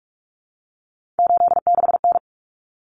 Morse code, heard as a keyed tone, about 44 wpm, 700 Hz. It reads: 86N
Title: Common Cut Numbers - 3-Digit 45wpm